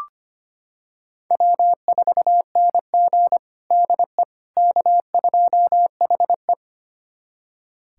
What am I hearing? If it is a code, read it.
Morse code, heard as a keyed tone, about 25 wpm, 700 Hz. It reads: W4NG DE K2HE
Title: USA Callsign DE Callsign - 4 Characters - 25wpm